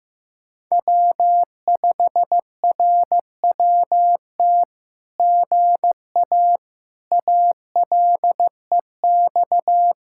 Morse code, heard as a keyed tone, about 15 wpm, 700 Hz. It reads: W5RWT GA ALEX
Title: SOTA QSO Practice 15wpm